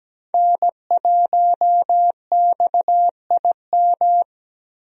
Morse code, heard as a keyed tone, about 17 wpm, 700 Hz. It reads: N1XIM
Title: USA Call Signs - 5 Characters - Repeated 2x - 17wpm